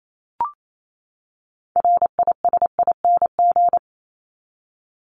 Morse code, heard as a keyed tone, about 28 wpm, 700 Hz. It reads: RISING
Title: ING Suffix - Top 100 Words with Base Words 28wpm